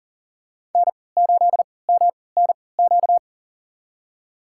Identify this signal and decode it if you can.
Morse code, heard as a keyed tone, about 40 wpm, spaced as for 23 wpm, 700 Hz. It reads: N8MNQ
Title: USA Call Signs - 3x Character Spacing - 40wpm